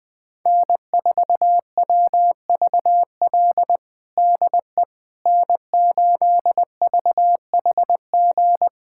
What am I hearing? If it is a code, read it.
Morse code, heard as a keyed tone, about 20 wpm, 700 Hz. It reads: N4WVL DE N8VHG
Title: USA Callsign DE Callsign - 5 Characters - 20wpm